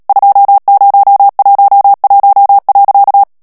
Morse code, heard as a keyed tone, about 37 wpm, 800 Hz. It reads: 2011.
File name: morse mp3 / ALLEMAND texte clair 2 vitesse 35 mots minute